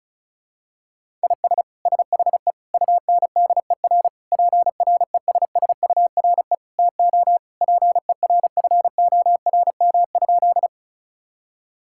Morse code, heard as a keyed tone, about 36 wpm, 700 Hz. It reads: IS SHE UNDER PRESSURE TO PERFORM?
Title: Sentences from Top 1000 Words - Review Part 6 - 35wpm